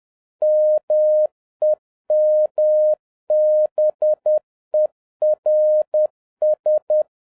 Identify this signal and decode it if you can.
Morse code, heard as a keyed tone, about 10 wpm, 610 Hz. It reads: MEMBERS
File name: 2011-07-mp3 / n46 JOHN K3TN 10wpm